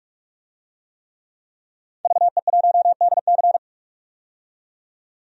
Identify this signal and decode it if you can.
Morse code, heard as a keyed tone, about 44 wpm, 690 Hz. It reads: VE1DC
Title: DXCC-100 Call Signs - 5 Characters - 45wpm